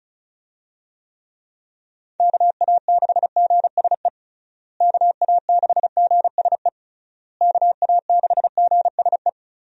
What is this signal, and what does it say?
Morse code, heard as a keyed tone, about 36 wpm, 700 Hz. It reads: KA6GSE KA6GSE KA6GSE
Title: USA Call Signs - 6 Characters - Repeated 3x - 35wpm